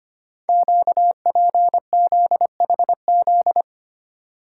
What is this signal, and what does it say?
Morse code, heard as a keyed tone, about 25 wpm, 700 Hz. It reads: QPZHZ
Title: Code Groups - 5 Letters - 25wpm